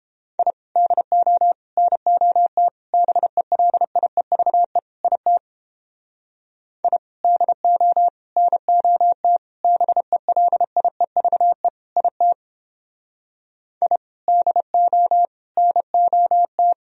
Morse code, heard as a keyed tone, about 31 wpm, 700 Hz. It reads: I DO NOT BELIEVE IT I DO NOT BELIEVE IT I DO NOT